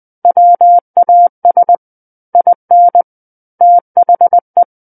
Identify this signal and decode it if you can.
Morse code, heard as a keyed tone, about 20 wpm, 700 Hz. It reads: WAS IN THE
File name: 20 WPM